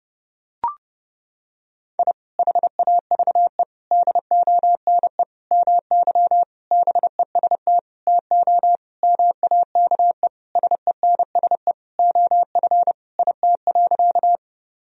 Morse code, heard as a keyed tone, about 32 wpm, 700 Hz. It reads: I HAVE DONE MY BEST TO MAKE SENSE OF IT.
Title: Sentences from Top 1000 Words - Review Part 4 - 30wpm